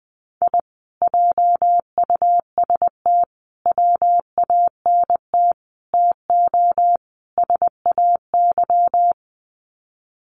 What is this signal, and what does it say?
Morse code, heard as a keyed tone, about 19 wpm, 700 Hz. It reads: I JUST WANT TO SAY
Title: Sets of 5 Words Encore 20wpm